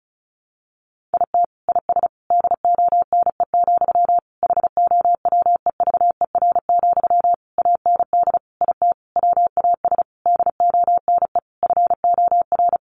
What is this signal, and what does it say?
Morse code, heard as a keyed tone, about 34 wpm, 700 Hz. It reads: IT IS DONE, HOWEVER, AND IT WAS DONE FOR
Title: Sentences from Top 600 Words 35wpm